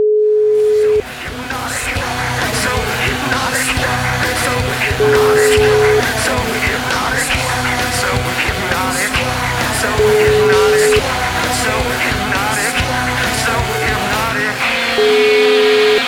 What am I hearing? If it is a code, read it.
Ringback tone (Europe, 425 Hz), with noise.